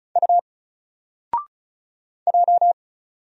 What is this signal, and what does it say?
Morse code, heard as a keyed tone, about 35 wpm, 700 Hz. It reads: U J
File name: Single-Letter-Number-35wpm